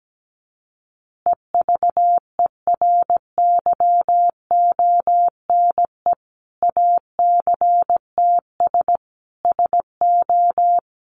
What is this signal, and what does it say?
Morse code, heard as a keyed tone, about 17 wpm, 700 Hz. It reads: EVERYONE ACTS SO
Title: Easy Phrases 17wpm